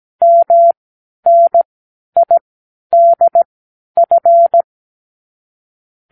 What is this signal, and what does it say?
Morse code, heard as a keyed tone, about 17 wpm, spaced as for 10 wpm, 680 Hz. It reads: MNIDF
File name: k7qo / Lesson 36 (A-X)